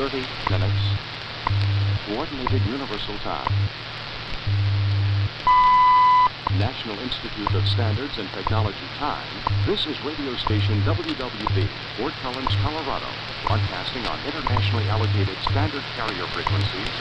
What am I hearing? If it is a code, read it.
Radio time signal, with speech.
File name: Radio / SDR Audio / wwv wwvh